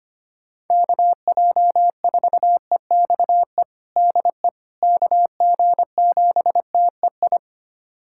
Morse code, heard as a keyed tone, about 25 wpm, 700 Hz. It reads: KJ4EXE DE KG7TEI
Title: USA Callsign DE Callsign - 6 Characters - 25wpm